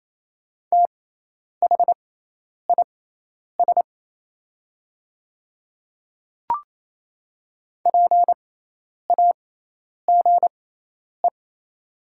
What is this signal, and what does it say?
Morse code, heard as a keyed tone, about 28 wpm, spaced as for 10 wpm, 700 Hz. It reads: THIS PAGE
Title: 6x Character Spacing - Top 300 Words - 28wpm